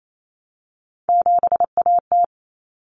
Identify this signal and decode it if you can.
Morse code, heard as a keyed tone, about 28 wpm, 700 Hz. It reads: 7AT
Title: Common Cut Numbers - 3-Digit 28wpm